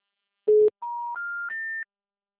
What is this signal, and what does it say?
Special information tone.